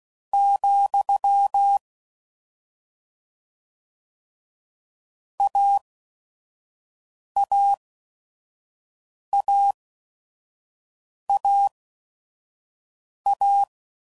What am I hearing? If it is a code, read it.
Morse code, heard as a keyed tone, about 16 wpm, spaced as for 5 wpm, 800 Hz. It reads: , AAAAA
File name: Combinations / Difficult Combos / AUV AUV4